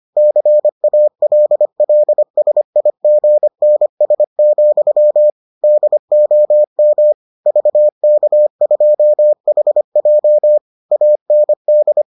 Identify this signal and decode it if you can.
Morse code, heard as a keyed tone, about 25 wpm, 600 Hz. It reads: CALLSIGNS, DOM VK2HJ AND